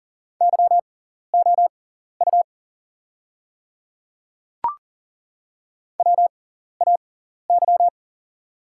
Morse code, heard as a keyed tone, about 40 wpm, spaced as for 14 wpm, 700 Hz. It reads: YOU WAY
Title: 6x Character Spacing - Top 200 Words - 40wpm